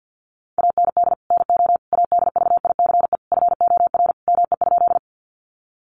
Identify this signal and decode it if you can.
Morse code, heard as a keyed tone, about 50 wpm, 700 Hz. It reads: AND NO ADVICE FOR ME?
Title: Sentences from Top 1100 Words 50wpm